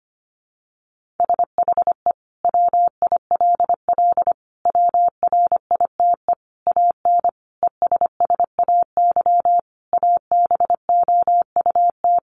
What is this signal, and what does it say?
Morse code, heard as a keyed tone, about 25 wpm, 700 Hz. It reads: SHE WILL WRITE AN ESSAY ABOUT
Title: Sentences from Top 1500 Words 25wpm